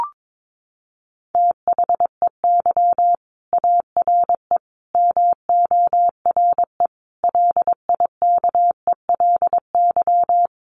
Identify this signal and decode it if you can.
Morse code, heard as a keyed tone, about 22 wpm, 700 Hz. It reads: THEY ARE MORE LIKELY
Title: Sets of 5 Words Encore2 22wpm